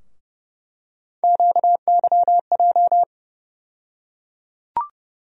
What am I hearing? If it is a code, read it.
Morse code, heard as a keyed tone, about 30 wpm, 700 Hz. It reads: QYJ
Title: USA Call Sign Postfixes - 3 Letters 30wpm